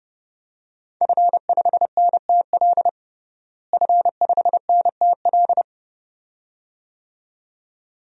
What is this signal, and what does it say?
Morse code, heard as a keyed tone, about 32 wpm, 700 Hz. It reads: F5NTL F5NTL
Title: DXCC-100 Call Signs - 5 Characters - Repeated 2x - 30wpm